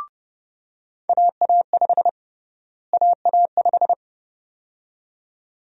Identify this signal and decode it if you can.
Morse code, heard as a keyed tone, about 30 wpm, 700 Hz. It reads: AA5 AA5
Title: USA Call Sign Prefixes - Repeated 2x - 30wpm